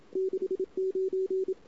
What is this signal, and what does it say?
Morse code, heard as a keyed tone, about 32 wpm, 410 Hz. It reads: 69
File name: CW / 2008-ss-cw-k5tr-op-n5rz -23:00